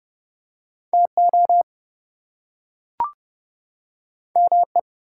Morse code, heard as a keyed tone, about 30 wpm, 700 Hz. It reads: TO ME
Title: Two Letter Words - 30wpm